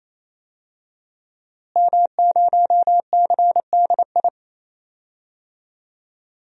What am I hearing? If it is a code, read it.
Morse code, heard as a keyed tone, about 28 wpm, 700 Hz. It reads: M0CDI
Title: DXCC-100 Call Signs - 5 Characters - Repeated 3x - 28wpm